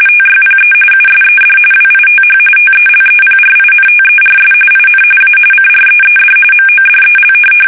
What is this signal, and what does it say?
RTTY, about 45.45 baud. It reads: TYPE USED WITH TELEPRINTERS IN A CREED MODEL
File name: fsk850 / teletype-RTTY